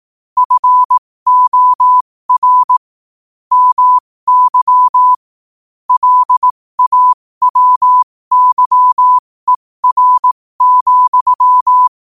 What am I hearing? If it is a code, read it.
Morse code, heard as a keyed tone, about 18 wpm, spaced as for 15 wpm, 1000 Hz. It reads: FOR MY LAWYER,
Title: 15 WPM morse code quotes for Thu, 29 Feb 2024